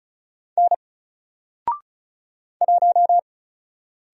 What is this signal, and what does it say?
Morse code, heard as a keyed tone, about 36 wpm, 700 Hz. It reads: N 1
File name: Single-Letter-Number-35wpm